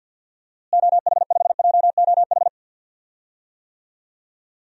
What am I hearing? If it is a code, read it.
Morse code, heard as a keyed tone, about 47 wpm, 700 Hz. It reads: OH5JOH
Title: DXCC-100 Call Signs - 6 Characters - Repeated 2x - 50wpm